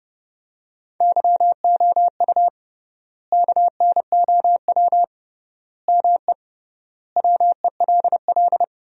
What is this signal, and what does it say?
Morse code, heard as a keyed tone, about 30 wpm, 700 Hz. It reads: YOU KNOW ME WELL
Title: 3x Word-Spacing - Sentences from Top 400 Words - 30wpm